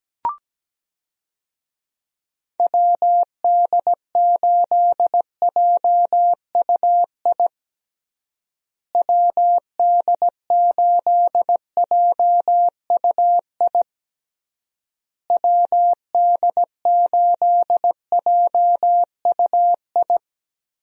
Morse code, heard as a keyed tone, about 17 wpm, 700 Hz. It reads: WD8JUI WD8JUI WD8JUI
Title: USA Call Signs - 6 Characters - Repeated 3x - 17wpm